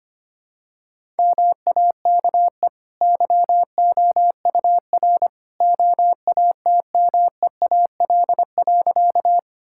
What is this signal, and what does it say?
Morse code, heard as a keyed tone, about 25 wpm, 690 Hz. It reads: MAKE YOUR OATMEAL.
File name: Easy Sentences with Hints - 25wpm